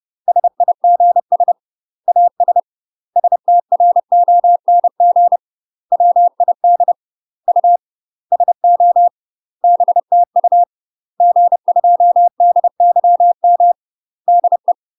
Morse code, heard as a keyed tone, about 30 wpm, 700 Hz. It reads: SIGS AS STRONG WID U SO BTU G2DYM DE
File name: UK MASTER QSOs TRAINING / 30WPM / UK MASTER QSOs 0021